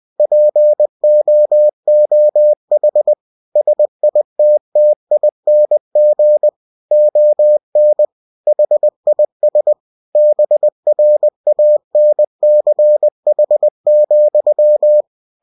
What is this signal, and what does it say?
Morse code, heard as a keyed tone, about 20 wpm, 600 Hz. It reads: POOH SITTING ON HIS BRANCH,
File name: cw-books / winnie-the-pooh / winnie-20wpm-page-0094